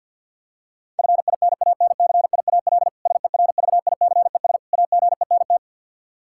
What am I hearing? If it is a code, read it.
Morse code, heard as a keyed tone, about 52 wpm, 700 Hz. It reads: FINANCIAL SERVICES AGENT
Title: Practice QSO - Exchange 3 - More Info - 50wpm